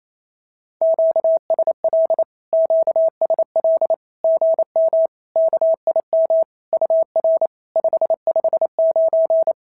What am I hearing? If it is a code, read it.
Morse code, heard as a keyed tone, about 28 wpm, 650 Hz. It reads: QSL QSL GM KIM UR 559
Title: SOTA QSO Practice 28wpm